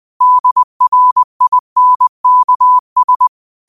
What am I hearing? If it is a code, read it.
Morse code, heard as a keyed tone, about 20 wpm, 1000 Hz. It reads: DRINKS
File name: 2024-09-13-20wpm